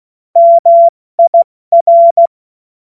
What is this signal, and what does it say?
Morse code, heard as a keyed tone, about 16 wpm, 680 Hz. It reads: MIR
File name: Morsekurs / Witze / Witz 10. Mai 2022 70 F80